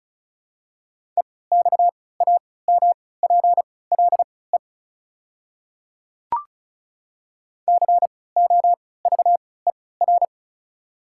Morse code, heard as a keyed tone, about 34 wpm, spaced as for 20 wpm, 700 Hz. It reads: EXAMPLE COVER
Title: 3x Character Spacing - Top 300 Words - 35wpm